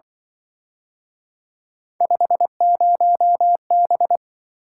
Morse code, heard as a keyed tone, about 25 wpm, 700 Hz. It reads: 50B
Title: Speed-Racing - Three Letter-Number Combinations - 20wpm